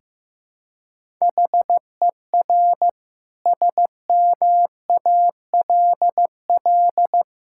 Morse code, heard as a keyed tone, about 14 wpm, 700 Hz. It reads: HER SMALL
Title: Sentences from Top 1000 Words - Review Part 6 - 15wpm